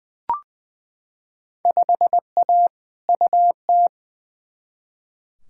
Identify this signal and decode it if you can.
Morse code, heard as a keyed tone, about 20 wpm, 700 Hz. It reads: 5A UT
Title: Field Day Exchanges - 20wpm